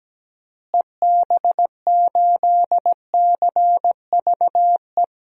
Morse code, heard as a keyed tone, about 17 wpm, 700 Hz. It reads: EB8CVE
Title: DXCC-100 Call Signs - 6 Characters - Repeated 3x - 17wpm